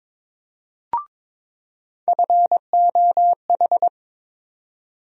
Morse code, heard as a keyed tone, about 22 wpm, 700 Hz. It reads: FOH
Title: USA Call Sign Postfixes - 3 Letters 22wpm